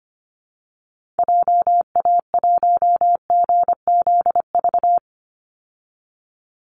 Morse code, heard as a keyed tone, about 25 wpm, 700 Hz. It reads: JA1GZV
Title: Call Signs Repeated 2x - 25wpm